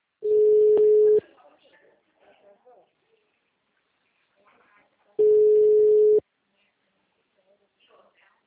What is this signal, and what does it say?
Ringback tone (Europe, 425 Hz), with speech.
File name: out- -2012- - - .157